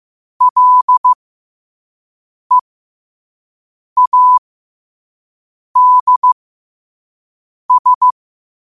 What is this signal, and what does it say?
Morse code, heard as a keyed tone, about 15 wpm, spaced as for 5 wpm, 1000 Hz. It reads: LEADS
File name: 5wpm